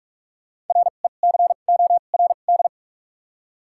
Morse code, heard as a keyed tone, about 41 wpm, spaced as for 31 wpm, 700 Hz. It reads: RECORD
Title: 2x Character Spacing - Top 500 Words - 45wpm